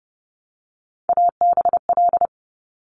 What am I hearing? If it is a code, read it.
Morse code, heard as a keyed tone, about 30 wpm, 700 Hz. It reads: ABL
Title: Top Three Letter Combinations in Words 30wpm